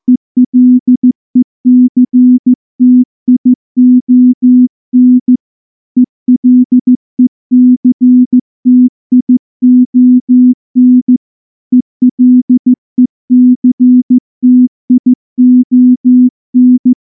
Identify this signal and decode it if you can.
Morse code, heard as a keyed tone, about 14 wpm, 250 Hz. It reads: ELECTION ELECTION ELECTION
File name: morse-training-15WPM